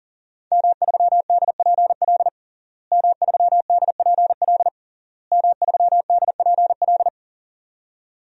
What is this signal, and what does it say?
Morse code, heard as a keyed tone, about 40 wpm, 700 Hz. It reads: M3DPL M3DPL M3DPL
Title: DXCC-100 Call Signs - 5 Characters - Repeated 3x - 40wpm